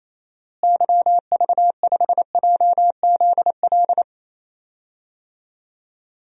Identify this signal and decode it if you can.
Morse code, heard as a keyed tone, about 28 wpm, 700 Hz. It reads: YV5JZL
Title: DXCC-100 Call Signs - 6 Characters - Repeated 2x - 28wpm